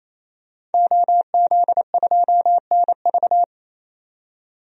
Morse code, heard as a keyed tone, about 28 wpm, 700 Hz. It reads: OZ2NV
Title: DXCC-100 Call Signs - 5 Characters - 28wpm